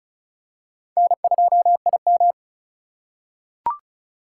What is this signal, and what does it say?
Morse code, heard as a keyed tone, about 34 wpm, 700 Hz. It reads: N2IM
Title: USA Call Signs - 4 Characters - 3x Character Spacing - 35wpm